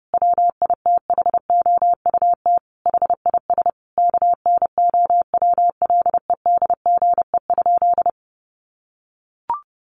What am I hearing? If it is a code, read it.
Morse code, heard as a keyed tone, about 30 wpm, 700 Hz. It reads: WITHOUT HIS KNOWLEDGE?
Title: Sentences from Top 600 Words 30wpm